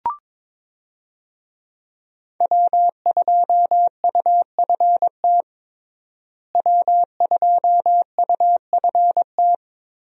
Morse code, heard as a keyed tone, about 22 wpm, 700 Hz. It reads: W2UFT W2UFT
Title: USA Call Signs - 5 Characters - Repeated 3x - 22wpm